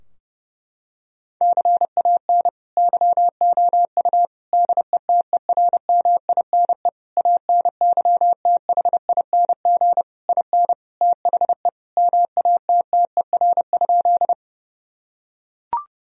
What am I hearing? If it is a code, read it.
Morse code, heard as a keyed tone, about 32 wpm, 700 Hz. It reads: CAN YOU DETERMINE ANYTHING IN THE MATTER?
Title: Sentences from Top 1000 Words - Review Part 4 - 30wpm